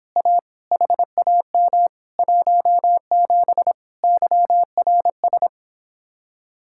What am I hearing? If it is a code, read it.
Morse code, heard as a keyed tone, about 26 wpm, 700 Hz. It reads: A HAM 17 YRS